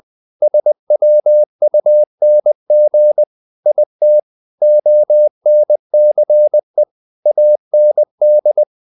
Morse code, heard as a keyed tone, about 20 wpm, 600 Hz. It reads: SWUNG IT ONCE AND
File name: the old man and the sea part 2 20wpm